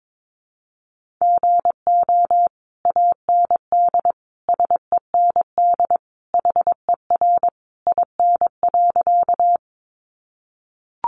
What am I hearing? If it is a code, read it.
Morse code, heard as a keyed tone, about 22 wpm, 700 Hz. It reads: GO AND SEND HER IN.
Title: Sentences from Top 1500 Words 22wpm